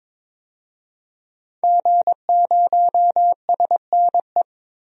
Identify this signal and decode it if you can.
Morse code, heard as a keyed tone, about 22 wpm, 700 Hz. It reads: G0SNE
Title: DXCC-100 Call Signs - 5 Characters - 22wpm